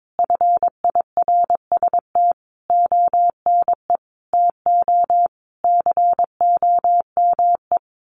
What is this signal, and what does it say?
Morse code, heard as a keyed tone, about 22 wpm, 700 Hz. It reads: FIRST ONE TO COME
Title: Sentences from Top 1300 Words 22wpm